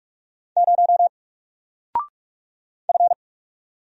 Morse code, heard as a keyed tone, about 46 wpm, 700 Hz. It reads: 0 F
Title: Single Letter-Number 45wpm